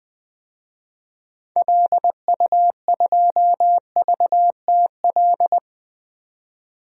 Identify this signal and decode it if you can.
Morse code, heard as a keyed tone, about 20 wpm, 700 Hz. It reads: LU2VTL
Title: DXCC-100 Call Signs - 6 Characters - 20wpm